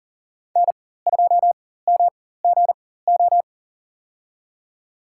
Morse code, heard as a keyed tone, about 40 wpm, spaced as for 19 wpm, 700 Hz. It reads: N2MGO
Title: USA Call Signs - 5 Characters - 4x Character Spacing - 40wpm